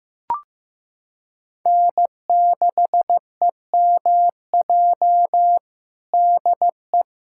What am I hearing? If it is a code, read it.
Morse code, heard as a keyed tone, about 15 wpm, 700 Hz. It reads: N6EMJ DE
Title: USA Callsign DE Callsign 15wpm